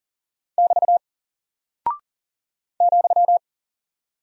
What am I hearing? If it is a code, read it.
Morse code, heard as a keyed tone, about 40 wpm, 700 Hz. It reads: <BT> ,
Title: Single Character (Letter:Number:Punctuation:Prosign) Alt 40wpm